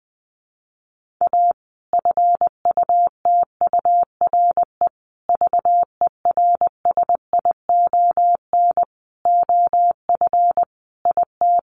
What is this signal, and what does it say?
Morse code, heard as a keyed tone, about 19 wpm, 700 Hz. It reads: A FUTURE VERSION OF IT
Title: Sentences from Top 1400 Words 20wpm